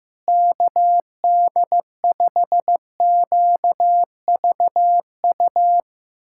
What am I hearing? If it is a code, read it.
Morse code, heard as a keyed tone, about 15 wpm, 700 Hz. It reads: KD5QVU
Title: USA Callsign DE Callsign - 6 Characters - 15wpm